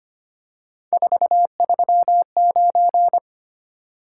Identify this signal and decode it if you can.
Morse code, heard as a keyed tone, about 25 wpm, 700 Hz. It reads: 439
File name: Three Letter-Number Combinations - 25wpm